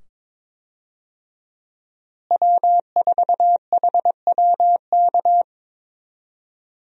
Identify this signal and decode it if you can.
Morse code, heard as a keyed tone, about 22 wpm, 700 Hz. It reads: W4HWK
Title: USA Call Signs - 5 Characters - Repeated 2x - 22wpm